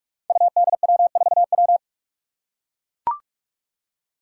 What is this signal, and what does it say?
Morse code, heard as a keyed tone, about 44 wpm, 700 Hz. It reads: UDW4W
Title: Five Challenging Letter-Number Combinations 45wpm